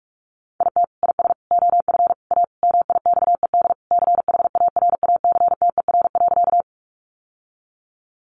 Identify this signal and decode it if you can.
Morse code, heard as a keyed tone, about 44 wpm, 700 Hz. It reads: IT IS OF A MIXED CHARACTER.
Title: Sentences from Top 1500 Words 45wpm